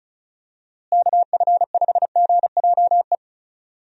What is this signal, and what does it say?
Morse code, heard as a keyed tone, about 36 wpm, 700 Hz. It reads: KF5GJE
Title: USA Call Signs - 6 Characters - Repeated 3x - 35wpm